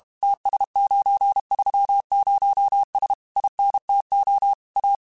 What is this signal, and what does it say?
Morse code, heard as a keyed tone, about 32 wpm, 800 Hz. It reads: TS930S INTO A